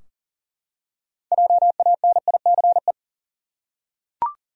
Morse code, heard as a keyed tone, about 40 wpm, 700 Hz. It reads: JANICE
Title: USA Common Names - 40wpm